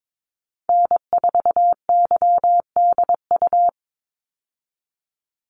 Morse code, heard as a keyed tone, about 22 wpm, 700 Hz. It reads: N4YDU
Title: Call Signs Repeated 3x - 22wpm